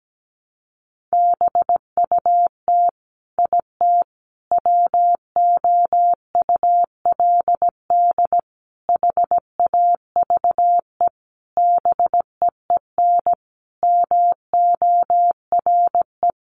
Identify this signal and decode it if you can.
Morse code, heard as a keyed tone, about 17 wpm, 700 Hz. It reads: BUT IT WOULD HAVE BEEN MORE
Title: Sentences from Top 1500 Words 17wpm